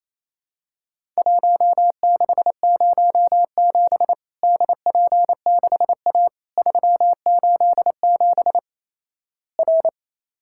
Morse code, heard as a keyed tone, about 28 wpm, 700 Hz. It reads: 1607 DP6A 387 R
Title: Worked All Europe DX Contest - QTC Exchange - 28wpm